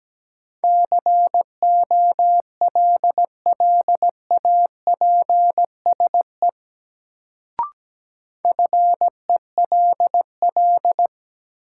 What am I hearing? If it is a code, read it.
Morse code, heard as a keyed tone, about 17 wpm, 700 Hz. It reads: COLLAPSE FELL
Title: Sets of 5 Words Encore - New Word Drill - 17wpm